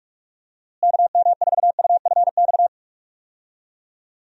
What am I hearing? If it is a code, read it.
Morse code, heard as a keyed tone, about 47 wpm, 700 Hz. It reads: KM4UFX
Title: USA Call Signs - 6 Characters - Repeated 2x - 45wpm